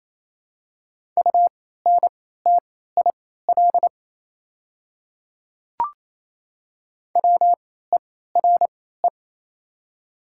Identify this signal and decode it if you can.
Morse code, heard as a keyed tone, about 28 wpm, spaced as for 16 wpm, 700 Hz. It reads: UNTIL WERE
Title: 3x Character Spacing - Top 300 Words - 28wpm